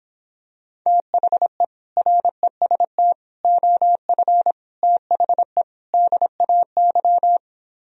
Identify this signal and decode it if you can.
Morse code, heard as a keyed tone, about 26 wpm, 700 Hz. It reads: THE REST OF THE DAY